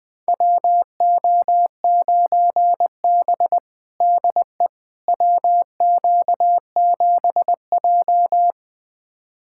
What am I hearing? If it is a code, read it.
Morse code, heard as a keyed tone, about 20 wpm, 700 Hz. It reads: WO9B DE WQ7J
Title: USA Callsign DE Callsign - 4 Characters - 20wpm